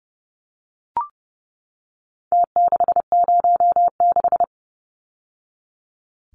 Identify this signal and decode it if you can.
Morse code, heard as a keyed tone, about 30 wpm, 700 Hz. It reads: T606
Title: Numbers - Contest Style - 4-Digit 30wpm